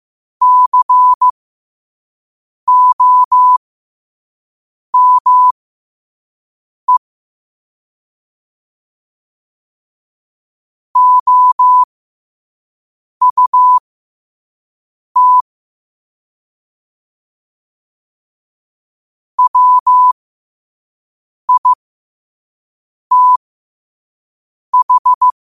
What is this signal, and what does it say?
Morse code, heard as a keyed tone, about 15 wpm, spaced as for 5 wpm, 1000 Hz. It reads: COME OUT WITH
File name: qotdfiles / 5wpm